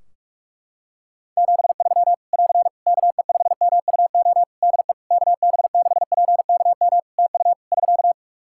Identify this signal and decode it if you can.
Morse code, heard as a keyed tone, about 46 wpm, 700 Hz. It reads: 73 <AR> KE5MUO DE KB6PKM TU <SK>
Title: Practice QSO - Exchange 4 - Ending - 45wpm